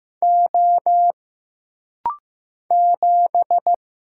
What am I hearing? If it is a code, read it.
Morse code, heard as a keyed tone, about 15 wpm, 700 Hz. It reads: O 7
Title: Single Letter-Number Alt - Rapid-Fire 15wpm